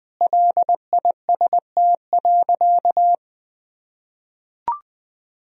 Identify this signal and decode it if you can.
Morse code, heard as a keyed tone, about 20 wpm, 700 Hz. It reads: LIST.
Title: Sentences from Top 1000 Words - Review Part 2 - 20wpm